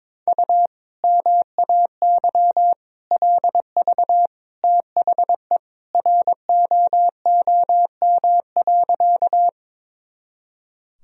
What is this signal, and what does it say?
Morse code, heard as a keyed tone, about 22 wpm, 700 Hz. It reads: U MAY LV THE ROOM.
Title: Sentences from Top 400 Words and Abbreviations 22wpm